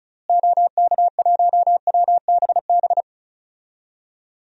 Morse code, heard as a keyed tone, about 36 wpm, 700 Hz. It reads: OK1WBB
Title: DXCC-100 Call Signs - 6 Characters - Repeated 2x - 35wpm